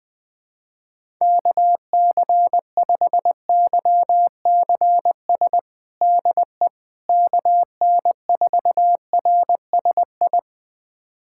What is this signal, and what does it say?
Morse code, heard as a keyed tone, about 20 wpm, 700 Hz. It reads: KC5YCS DE KN4RSI
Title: USA Callsign DE Callsign - 6 Characters - 20wpm